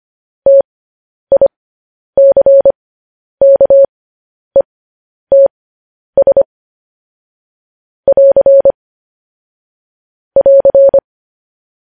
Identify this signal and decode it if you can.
Morse code, heard as a keyed tone, about 25 wpm, spaced as for 10 wpm, 550 Hz. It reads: TICKETS <AR> <AR>
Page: news-headlines-in-morse-code-at-10-wpm